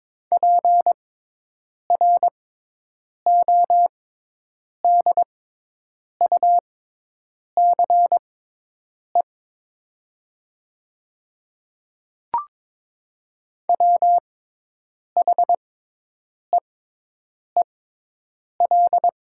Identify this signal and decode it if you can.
Morse code, heard as a keyed tone, about 22 wpm, spaced as for 8 wpm, 700 Hz. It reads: PRODUCE WHEEL
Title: 6x Character Spacing - Top 500 Words - 22wpm